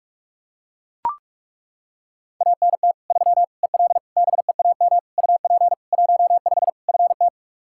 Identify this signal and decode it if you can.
Morse code, heard as a keyed tone, about 41 wpm, 700 Hz. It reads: ANT 3 EL BEAM UP 15 FT
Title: Practice QSO - Exchange 2 - Info - 45wpm